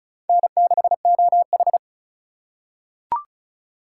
Morse code, heard as a keyed tone, about 36 wpm, 700 Hz. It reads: N6OH
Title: USA Call Signs - 4 Characters - 3x Character Spacing - 35wpm